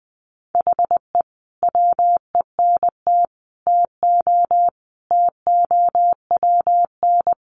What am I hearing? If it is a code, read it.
Morse code, heard as a keyed tone, about 20 wpm, 700 Hz. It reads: HE WENT TO TOWN
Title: Sentences from Top 1300 Words 20wpm